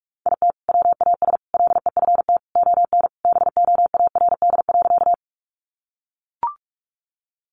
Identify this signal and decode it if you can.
Morse code, heard as a keyed tone, about 44 wpm, 700 Hz. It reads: IT WAS LEFT ON BOARD.
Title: Sentences from Top 700 Words 45wpm